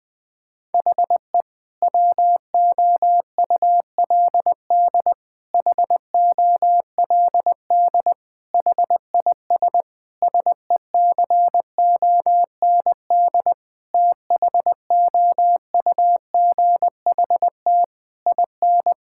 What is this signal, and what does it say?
Morse code, heard as a keyed tone, about 19 wpm, 700 Hz. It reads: HE WOULD HOLD HIS SECOND THOUGHT IN
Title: Sentences from Top 1000 Words - Review Part 6 - 20wpm